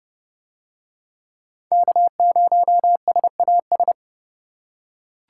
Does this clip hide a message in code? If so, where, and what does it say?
Yes: Morse code from 1.7 to 3.9 s, heard as a keyed tone, about 26 wpm, 700 Hz. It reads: K0SAS